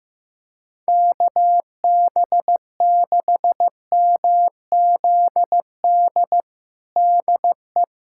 Morse code, heard as a keyed tone, about 15 wpm, 700 Hz. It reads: KB6MZD DE
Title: USA Callsign DE Callsign - 6 Characters - 15wpm